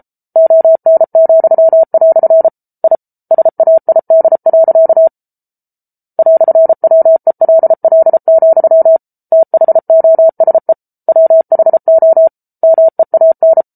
Morse code, heard as a keyed tone, about 34 wpm, 650 Hz. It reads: ON," I SAID. "WELL, THOSE WHO MEAN